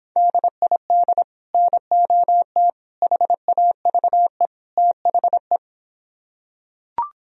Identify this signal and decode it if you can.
Morse code, heard as a keyed tone, about 26 wpm, 700 Hz. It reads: DID NOT HAVE THE